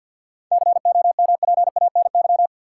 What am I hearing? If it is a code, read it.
Morse code, heard as a keyed tone, about 53 wpm, 700 Hz. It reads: COMPANY
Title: Seven Letter Words - 50wpm